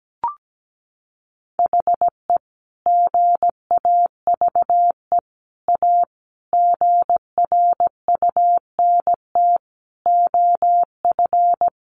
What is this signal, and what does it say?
Morse code, heard as a keyed tone, about 17 wpm, 700 Hz. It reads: HE GAVE A GRUNT OF
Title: A Sound Of Thunder 17wpm